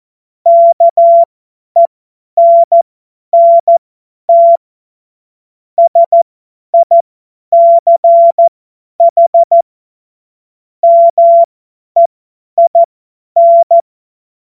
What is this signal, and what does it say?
Morse code, heard as a keyed tone, about 14 wpm, spaced as for 10 wpm, 680 Hz. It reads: KENNT SICH MEIN
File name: Morsekurs / Witze / Witz 11. Januar 2022, 50 F70